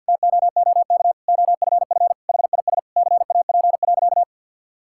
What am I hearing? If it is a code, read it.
Morse code, heard as a keyed tone, about 50 wpm, 700 Hz. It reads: TOOK OFF HIS CAP.
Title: Sentences from Top 1600 Words 50wpm